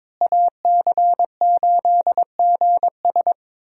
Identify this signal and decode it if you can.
Morse code, heard as a keyed tone, about 21 wpm, 700 Hz. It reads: AC8GS
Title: USA Call Signs - 5 Characters - Repeated 2x - 22wpm